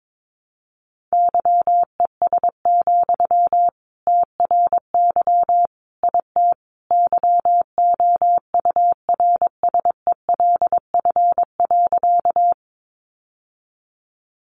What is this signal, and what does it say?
Morse code, heard as a keyed tone, about 22 wpm, 700 Hz. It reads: YES, TRY IT YOURSELF.
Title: Sentences from Top 600 Words 22wpm